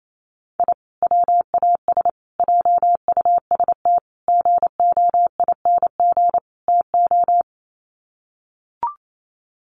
Morse code, heard as a keyed tone, about 28 wpm, 700 Hz. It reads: I WAS JUST GOING TO
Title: Sets of 5 Words Encore 28wpm